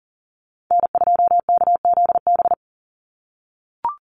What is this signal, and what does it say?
Morse code, heard as a keyed tone, about 40 wpm, 700 Hz. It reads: N2KZB
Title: USA Call Signs 40wpm